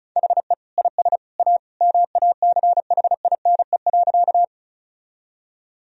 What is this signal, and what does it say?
Morse code, heard as a keyed tone, about 36 wpm, 700 Hz. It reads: HE IS A MACHINE.
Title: Sentences from Top 1000 Words - Review Part 2 - 35wpm